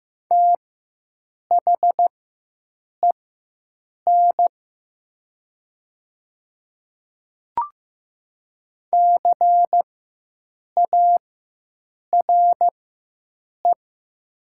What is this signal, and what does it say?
Morse code, heard as a keyed tone, about 15 wpm, spaced as for 7 wpm, 700 Hz. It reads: THEN CARE
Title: 4x Character Spacing - Top 300 Words - 15wpm